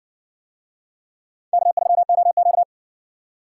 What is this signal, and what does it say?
Morse code, heard as a keyed tone, about 60 wpm, 700 Hz. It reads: K3OQ